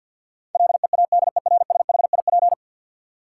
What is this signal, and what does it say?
Morse code, heard as a keyed tone, about 52 wpm, 700 Hz. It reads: LEADERSHIP
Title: Sentences from Top 1600 Words - New Word Drill Alt - 50wpm